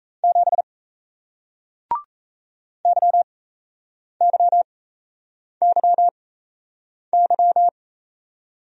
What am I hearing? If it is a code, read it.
Morse code, heard as a keyed tone, about 37 wpm, spaced as for 8 wpm, 700 Hz. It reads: 7 YYYY